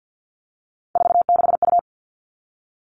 Morse code, heard as a keyed tone, about 50 wpm, 700 Hz. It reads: 46U
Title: All Cut Numbers - 3-Digit 50wpm